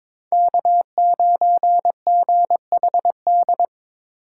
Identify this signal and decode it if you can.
Morse code, heard as a keyed tone, about 21 wpm, 700 Hz. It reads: K9GHD
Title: USA Call Signs - 5 Characters - 4x Character Spacing - 22wpm